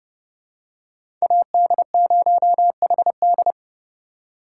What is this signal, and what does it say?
Morse code, heard as a keyed tone, about 30 wpm, 700 Hz. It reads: AD0HD
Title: USA Call Signs - 5 Characters - 2x Character Spacing - 30wpm